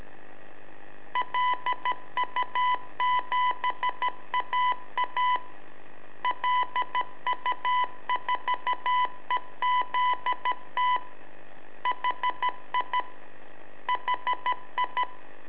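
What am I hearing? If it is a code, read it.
Morse code, heard as a keyed tone, about 13 wpm, 1000 Hz. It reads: LU7AA LU4EZT HI HI
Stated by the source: Telemetry in Morse from LU7AA and LU4EZ, AMSAT and Turma Mar del Plata, sent from a balloon.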